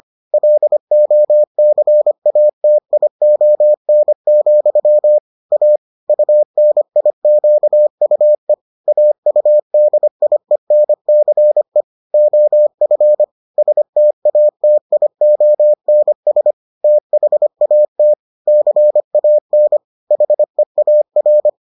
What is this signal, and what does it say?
Morse code, heard as a keyed tone, about 25 wpm, 600 Hz. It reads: LOCATION, A UNIQUE AUDIENCE OF STATIONS THAT CAN HEAR